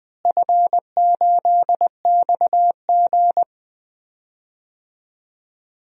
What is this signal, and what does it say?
Morse code, heard as a keyed tone, about 20 wpm, 700 Hz. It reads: F8XG
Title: DXCC-100 Call Signs - 4 Characters - Repeated 2x - 20wpm